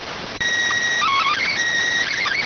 SSTV (slow-scan television), PD 290.